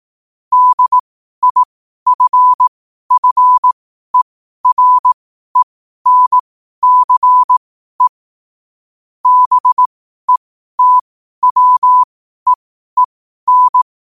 Morse code, heard as a keyed tone, about 18 wpm, spaced as for 12 wpm, 1000 Hz. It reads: DIFFERENCE BETWEEN
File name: qotdfiles / 12wpm